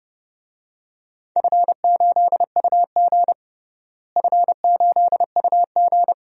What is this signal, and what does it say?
Morse code, heard as a keyed tone, about 32 wpm, 700 Hz. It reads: F8UG F8UG
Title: DXCC-100 Call Signs - 4 Characters - Repeated 2x - 30wpm